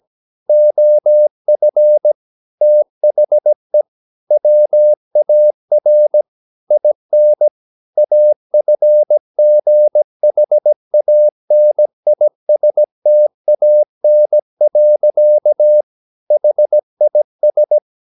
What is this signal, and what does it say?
Morse code, heard as a keyed tone, about 17 wpm, 600 Hz. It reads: OF THE WAR IN AFGHANISTAN. HIS
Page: Romanen Never af Ken Follet på engelsk.